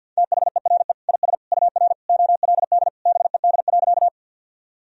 Morse code, heard as a keyed tone, about 52 wpm, 700 Hz. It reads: THERE IS UR OLD BED.
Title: Sentences from Top 500 Words and Abbreviations 50wpm